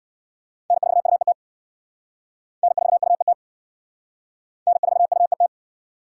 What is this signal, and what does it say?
Morse code, heard as a keyed tone, about 58 wpm, 690 Hz. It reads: N4UET N4UET N4UET
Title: Speed Racing - USA 1x3 Call Signs 50wpm